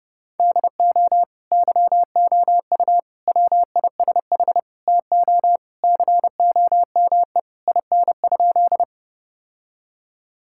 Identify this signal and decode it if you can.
Morse code, heard as a keyed tone, about 32 wpm, 700 Hz. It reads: DO YOU WISH TO COME IN?
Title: Sentences from Top 1600 Words 30wpm